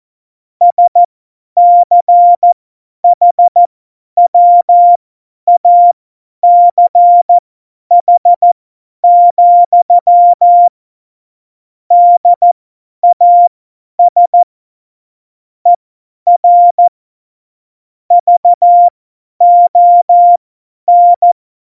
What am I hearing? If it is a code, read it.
Morse code, heard as a keyed tone, about 14 wpm, spaced as for 10 wpm, 700 Hz. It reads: SCHWACH, DAS ER VON
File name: Morsekurs / Witze / Witz 07. Januar 2020, 50 F70